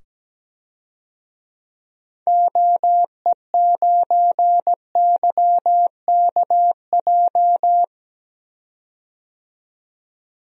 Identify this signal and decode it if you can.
Morse code, heard as a keyed tone, about 17 wpm, 700 Hz. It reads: OE9YKJ
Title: DXCC-100 Call Signs - 6 Characters - 17wpm